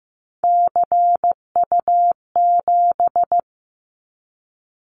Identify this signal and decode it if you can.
Morse code, heard as a keyed tone, about 15 wpm, 700 Hz. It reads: CU7
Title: Callsign Prefixes for Top 100 DXCC - 15wpm